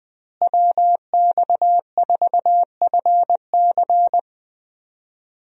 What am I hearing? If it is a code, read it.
Morse code, heard as a keyed tone, about 21 wpm, 700 Hz. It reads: WX4FC
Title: USA Call Signs - 5 Characters - Repeated 3x - 20wpm